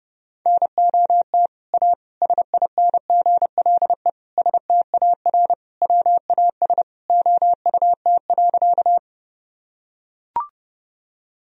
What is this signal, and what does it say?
Morse code, heard as a keyed tone, about 32 wpm, 700 Hz. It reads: NOT A SINGLE STAR WAS OUT.
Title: Sentences from Top 1000 Words - Review Part 4 - 30wpm